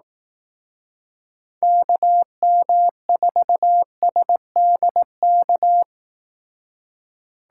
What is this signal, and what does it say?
Morse code, heard as a keyed tone, about 18 wpm, 700 Hz. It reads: KM4SDK
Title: Speed Racing - USA 2x3 Call Signs 15wpm